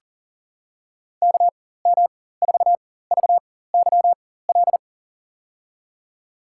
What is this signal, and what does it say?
Morse code, heard as a keyed tone, about 40 wpm, spaced as for 19 wpm, 700 Hz. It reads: KM4VYL
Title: USA Call Signs - 6 Characters - 4x Character Spacing - 40wpm